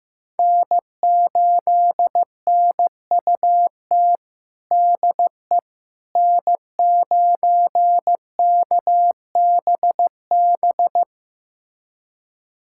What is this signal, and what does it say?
Morse code, heard as a keyed tone, about 15 wpm, 700 Hz. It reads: N8NUT DE N9KBB
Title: USA Callsign DE Callsign - 5 Characters - 15wpm